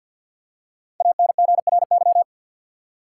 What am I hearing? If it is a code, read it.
Morse code, heard as a keyed tone, about 47 wpm, 700 Hz. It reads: ANGRY
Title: Five Letter Words - 50wpm